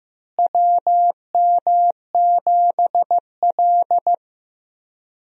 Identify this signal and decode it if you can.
Morse code, heard as a keyed tone, about 15 wpm, 700 Hz. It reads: WM7L
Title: USA Call Signs - 4 Characters - Repeated 3x - 15wpm